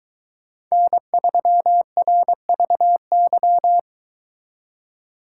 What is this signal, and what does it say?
Morse code, heard as a keyed tone, about 23 wpm, 700 Hz. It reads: N3RVY